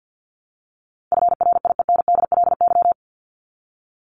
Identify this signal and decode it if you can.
Morse code, heard as a keyed tone, about 50 wpm, 700 Hz. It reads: FRIENDLY
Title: LY Suffix - Top 100 Words with Base Words 50wpm